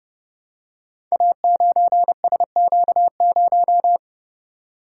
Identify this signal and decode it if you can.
Morse code, heard as a keyed tone, about 30 wpm, 700 Hz. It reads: A9SQ0
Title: Code Groups - 5 Letter:Numbers - 30wpm